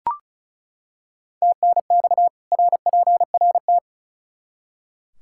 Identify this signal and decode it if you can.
Morse code, heard as a keyed tone, about 36 wpm, 700 Hz. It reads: TNX RPRT
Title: Practice QSO - Exchange 2 - Info - 35wpm